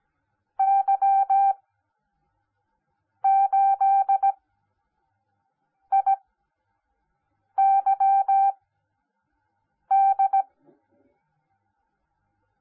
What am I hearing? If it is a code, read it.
Morse code, heard as a keyed tone, about 17 wpm, spaced as for 5 wpm, 780 Hz. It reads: Y8IYD
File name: Onlinelehrgang / morsen / DJ4UF Morsen10